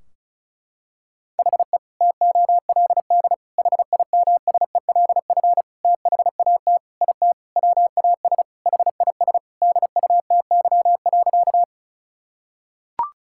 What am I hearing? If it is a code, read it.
Morse code, heard as a keyed tone, about 34 wpm, 700 Hz. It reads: HE TOLD HIMSELF THAT IT WAS HIS DUTY.
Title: Sentences from Top 1600 Words 35wpm